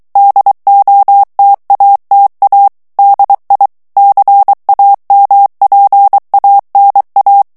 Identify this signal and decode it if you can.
Morse code, heard as a keyed tone, about 23 wpm, 800 Hz. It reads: DOTATA DI CAMPANA
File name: ITALIEN texte clair 1 vitesse 22 mots minute